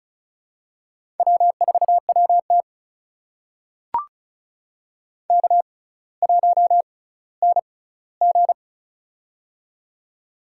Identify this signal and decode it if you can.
Morse code, heard as a keyed tone, about 34 wpm, spaced as for 18 wpm, 700 Hz. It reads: W4WT K1NG
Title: USA Call Signs - 4 Characters - 6x Character Spacing - 35wpm